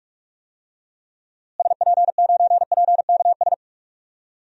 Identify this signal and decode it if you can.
Morse code, heard as a keyed tone, about 41 wpm, 700 Hz. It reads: SP9PKS
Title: DXCC-100 Call Signs - 6 Characters - Repeated 2x - 45wpm